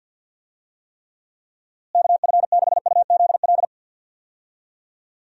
Morse code, heard as a keyed tone, about 51 wpm, 700 Hz. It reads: KF6UZL
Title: USA Call Signs - 6 Characters - Repeated 2x - 50wpm